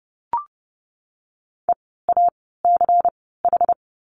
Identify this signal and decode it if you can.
Morse code, heard as a keyed tone, about 30 wpm, spaced as for 17 wpm, 700 Hz. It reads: EACH
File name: Lesson 17f - Letter u - Words 30wpm